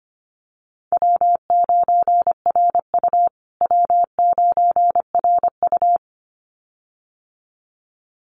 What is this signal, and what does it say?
Morse code, heard as a keyed tone, about 25 wpm, 700 Hz. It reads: W9RU W9RU
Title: USA Call Signs Repeated 2x 25wpm